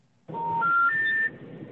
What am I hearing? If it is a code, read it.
Special information tone.